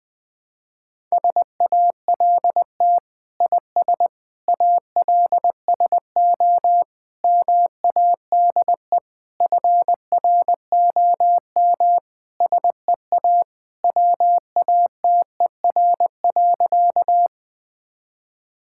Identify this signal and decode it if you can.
Morse code, heard as a keyed tone, about 20 wpm, 700 Hz. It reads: SALT IS ALSO MADE FROM SEA WATER.
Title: Sentences from Top 1000 Words - Review Part 5 - 20wpm